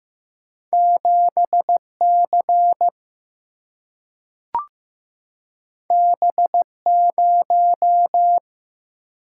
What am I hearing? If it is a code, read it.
Morse code, heard as a keyed tone, about 15 wpm, 700 Hz. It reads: 7C B0
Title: Single-Letter Single-Number 15wpm